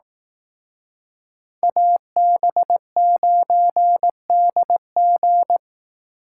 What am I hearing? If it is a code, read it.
Morse code, heard as a keyed tone, about 18 wpm, 700 Hz. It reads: AB9DG
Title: Speed Racing - USA 2x2 Call Signs 15wpm